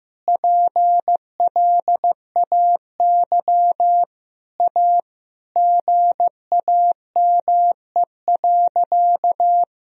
Morse code, heard as a keyed tone, about 15 wpm, 700 Hz. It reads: PLAY A GAME.
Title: Sentences from Top 1000 Words - Review Part 4 - 15wpm